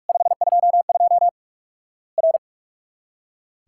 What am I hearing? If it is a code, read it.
Morse code, heard as a keyed tone, about 44 wpm, 700 Hz. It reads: 522 R
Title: Worked All Europe DX Contest - QTC Exchange - 45wpm